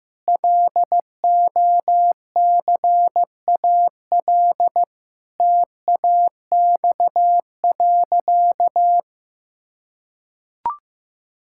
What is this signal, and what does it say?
Morse code, heard as a keyed tone, about 15 wpm, 700 Hz. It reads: LOCAL TAX.
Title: Sentences from Top 1000 Words - Review Part 5 - 15wpm